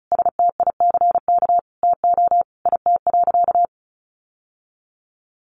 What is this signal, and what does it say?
Morse code, heard as a keyed tone, about 34 wpm, 700 Hz. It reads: STICK TO IT.
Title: Sentences from Top 1200 Words 35wpm